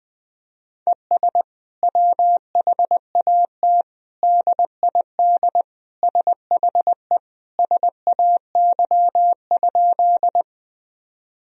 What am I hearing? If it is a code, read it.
Morse code, heard as a keyed tone, about 21 wpm, 700 Hz. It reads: ES WHAT DID SHE SAY?
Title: Sentences from Top 200 Words and Abbreviations 20wpm